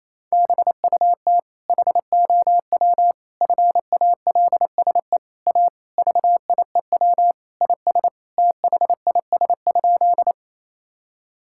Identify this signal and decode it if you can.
Morse code, heard as a keyed tone, about 28 wpm, 700 Hz. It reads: BUT HOW FALSE A VIEW IS THIS?
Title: Sentences from Top 1600 Words 28wpm